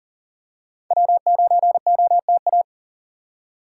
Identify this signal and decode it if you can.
Morse code, heard as a keyed tone, about 40 wpm, 700 Hz. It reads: W9OTA
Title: USA Call Signs - 5 Characters - 4x Character Spacing - 40wpm